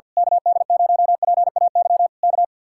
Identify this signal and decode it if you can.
Morse code, heard as a keyed tone, about 49 wpm, 700 Hz. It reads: KD0PAY K
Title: Speed-Racing Common QSO Phrases Encore - 50wpm